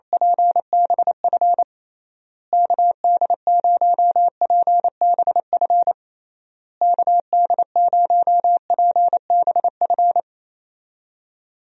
Morse code, heard as a keyed tone, about 28 wpm, 700 Hz. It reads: PBF KD0PBF KD0PBF
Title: USA Call Signs - 6 Characters - Repeated 3x - 28wpm